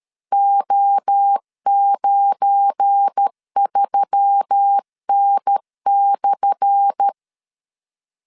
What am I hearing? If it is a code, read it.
Morse code, heard as a keyed tone, about 13 wpm, 800 Hz. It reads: O93N/
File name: LEZIONI / 10 SIGNS / 10A SLASH 12wpm 2314